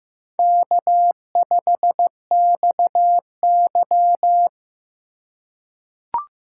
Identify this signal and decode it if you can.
Morse code, heard as a keyed tone, about 15 wpm, 700 Hz. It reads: K5XY
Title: USA Call Signs - 4 Characters - 3x Character Spacing - 15wpm